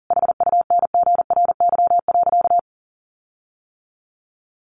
Morse code, heard as a keyed tone, about 40 wpm, 700 Hz. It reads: HUNGRY.
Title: Sentences from Top 1400 Words 40wpm